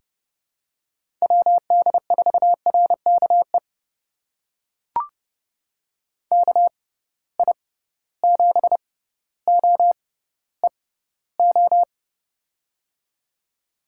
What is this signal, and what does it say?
Morse code, heard as a keyed tone, about 30 wpm, spaced as for 15 wpm, 700 Hz. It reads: WD4RKE KI7OEO
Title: USA Call Signs - 6 Characters - 6x Character Spacing - 30wpm